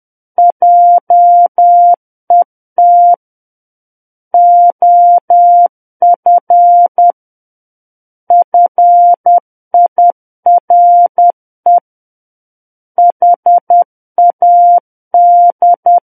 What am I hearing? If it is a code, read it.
Morse code, heard as a keyed tone, about 10 wpm, 700 Hz. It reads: JET OF FIRE HAD
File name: Book One / 01-Chapter 1 - 10 WPM